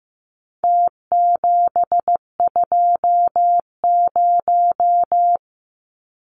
Morse code, heard as a keyed tone, about 15 wpm, 700 Hz. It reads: T720
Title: Numbers - Contest Style - 4-Digit 15wpm